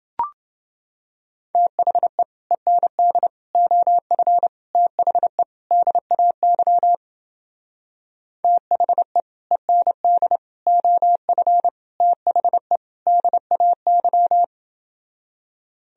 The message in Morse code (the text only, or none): THE END OF THE DAY THE END OF THE DAY